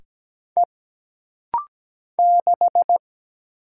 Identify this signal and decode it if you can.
Morse code, heard as a keyed tone, about 17 wpm, 700 Hz. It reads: E 6
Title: Single Letter-Number - Rapid-Fire 17wpm